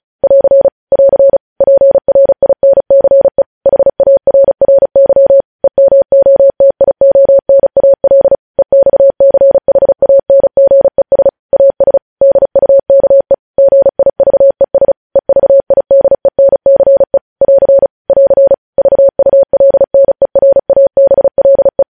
Morse code, heard as a keyed tone, about 36 wpm, 550 Hz. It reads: <AR> <AR> PRINCE HARRY EMOTIONAL EXCHANGES AS DUKE GIVES EVIDENCE <AR> <AR> VULNERABLE
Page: News Headlines in Morse Code at 35 WPM